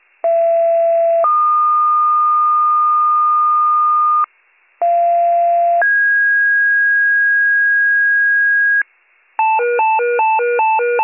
Two-tone page, heard as a one-second tone, then a three-second tone, about 670 then 1220 Hz, about 690 then 1650 Hz, with noise. It is heard from 0.2 to 8.8 s.